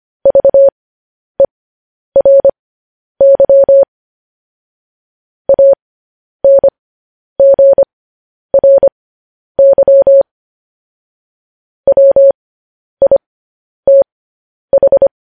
Morse code, heard as a keyed tone, about 25 wpm, spaced as for 10 wpm, 550 Hz. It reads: VERY ANGRY WITH
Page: news-headlines-in-morse-code-at-10-wpm